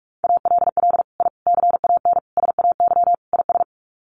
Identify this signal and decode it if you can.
Morse code, heard as a keyed tone, about 44 wpm, 690 Hz. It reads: ALL I CAN SAY IS
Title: Sets of 5 Words Encore2 45wpm